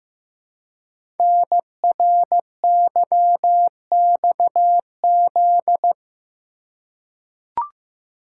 Morse code, heard as a keyed tone, about 15 wpm, 700 Hz. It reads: NRYXZ
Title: Code Groups - 5 Letters - 15wpm